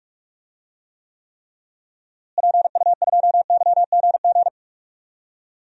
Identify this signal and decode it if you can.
Morse code, heard as a keyed tone, about 41 wpm, 700 Hz. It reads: PU2YGG
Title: DXCC-100 Call Signs - 6 Characters - Repeated 3x - 45wpm